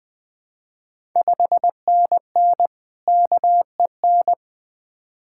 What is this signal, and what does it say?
Morse code, heard as a keyed tone, about 20 wpm, 700 Hz. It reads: 5NN KEN
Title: Ontario QSO Party Exchange 20wpm